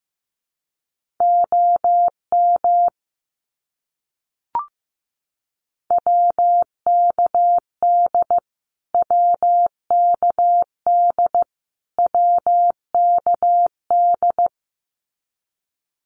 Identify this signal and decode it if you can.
Morse code, heard as a keyed tone, about 15 wpm, 700 Hz. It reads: OM WKD WKD WKD
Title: Abbreviations Repeated 3x - 15wpm